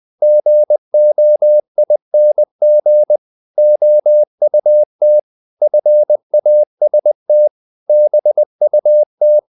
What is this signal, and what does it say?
Morse code, heard as a keyed tone, about 20 wpm, 600 Hz. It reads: GOING OUT FAST BUT
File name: the old man and the sea / the old man and the sea part 1 20wpm